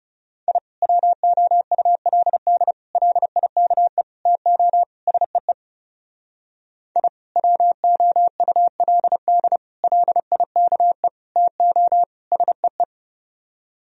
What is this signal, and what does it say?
Morse code, heard as a keyed tone, about 32 wpm, 700 Hz. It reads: I WOULD LIKE TO SEE I WOULD LIKE TO SEE
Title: Speed-Racing - Sets of 5 Words 30wpm